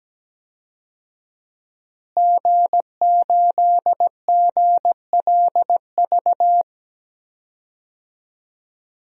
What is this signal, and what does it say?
Morse code, heard as a keyed tone, about 17 wpm, 700 Hz. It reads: G8GLV
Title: DXCC-100 Call Signs - 5 Characters - Repeated 3x - 17wpm